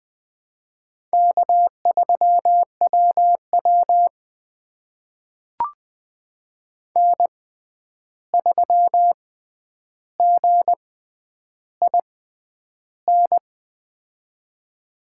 Morse code, heard as a keyed tone, about 21 wpm, spaced as for 10 wpm, 700 Hz. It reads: K3WW N3GIN
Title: USA Call Signs - 6x Character Spacing - 20wpm